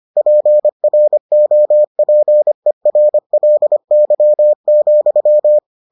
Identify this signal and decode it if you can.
Morse code, heard as a keyed tone, about 25 wpm, 600 Hz. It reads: PROPERLY,